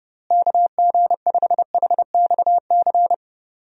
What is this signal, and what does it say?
Morse code, heard as a keyed tone, about 30 wpm, 700 Hz. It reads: KG5HXC
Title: USA Call Signs - 6 Characters - Repeated 3x - 30wpm